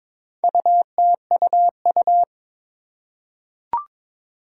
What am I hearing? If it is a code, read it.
Morse code, heard as a keyed tone, about 22 wpm, 700 Hz. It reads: UTUU
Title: Four Challenging Letter-Number Combinations 22wpm